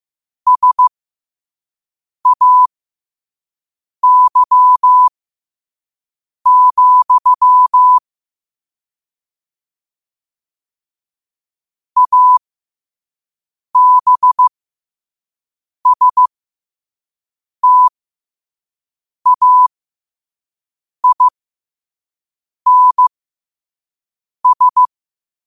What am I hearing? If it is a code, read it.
Morse code, heard as a keyed tone, about 15 wpm, spaced as for 5 wpm, 1000 Hz. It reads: SAY, ABSTAINS
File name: qotdfiles / 5wpm